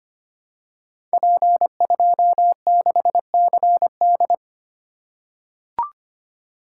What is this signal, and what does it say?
Morse code, heard as a keyed tone, about 25 wpm, 700 Hz. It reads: P26CD
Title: Code Groups - 5 Letter:Numbers - 25wpm